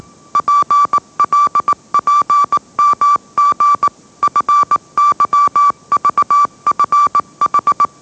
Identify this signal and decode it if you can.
Morse code, heard as a keyed tone, about 18 wpm, 1160 Hz. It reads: PLPMG FYVFH